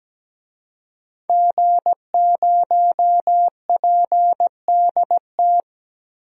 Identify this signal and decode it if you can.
Morse code, heard as a keyed tone, about 17 wpm, 700 Hz. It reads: G0PDT
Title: DXCC-100 Call Signs - 5 Characters - 17wpm